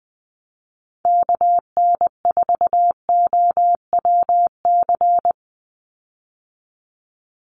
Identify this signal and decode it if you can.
Morse code, heard as a keyed tone, about 20 wpm, 700 Hz. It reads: KN4OWC
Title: USA Call Signs 20wpm